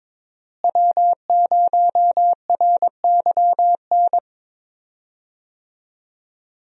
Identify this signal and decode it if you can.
Morse code, heard as a keyed tone, about 22 wpm, 700 Hz. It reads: W0RYN
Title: USA Call Signs - 5 Characters - Repeated 3x - 22wpm